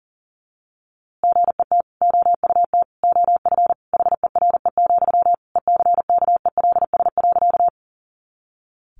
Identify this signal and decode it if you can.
Morse code, heard as a keyed tone, about 40 wpm, 700 Hz. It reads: GET OUT OF HERE, ECKELS.
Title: A Sound Of Thunder 40wpm